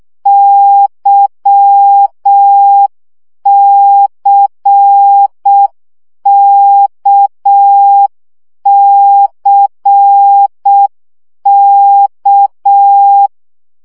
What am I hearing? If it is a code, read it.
Morse code, heard as a keyed tone, about 6 wpm, 800 Hz. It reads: YCKCK
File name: CW gyakorlas k c y 5 baud